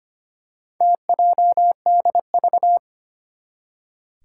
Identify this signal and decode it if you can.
Morse code, heard as a keyed tone, about 25 wpm, 700 Hz. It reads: TJDV
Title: Four Challenging Letter-Number Combinations 25wpm